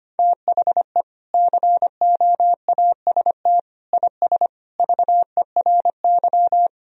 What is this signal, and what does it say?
Morse code, heard as a keyed tone, about 25 wpm, 700 Hz. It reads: THE COAST IS VERY
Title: Sentences from Top 1000 Words - Review Part 4 - 25wpm